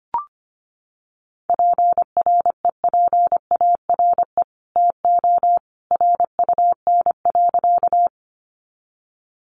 Morse code, heard as a keyed tone, about 25 wpm, 700 Hz. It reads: PREPARE TO RUN.
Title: Sentences from Top 900 Words 25wpm